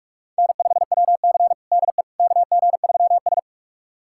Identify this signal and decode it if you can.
Morse code, heard as a keyed tone, about 41 wpm, 700 Hz. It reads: N5WC DE KG3S
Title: USA Callsign DE Callsign - 4 Characters - 45wpm